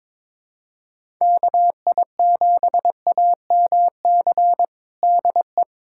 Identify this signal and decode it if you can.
Morse code, heard as a keyed tone, about 21 wpm, 700 Hz. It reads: KI7AMC DE
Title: USA Callsign DE Callsign 22wpm